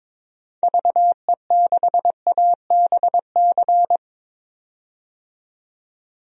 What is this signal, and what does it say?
Morse code, heard as a keyed tone, about 23 wpm, 700 Hz. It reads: VE6ABC
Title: DXCC-100 Call Signs - 6 Characters - 22wpm